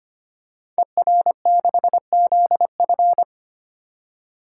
Morse code, heard as a keyed tone, about 25 wpm, 700 Hz. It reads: ER6ZF
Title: Code Groups - 5 Letter:Numbers - 25wpm